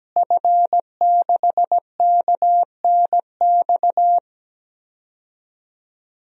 Morse code, heard as a keyed tone, about 17 wpm, 700 Hz. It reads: F6KNX
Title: DXCC-100 Call Signs - 5 Characters - Repeated 3x - 17wpm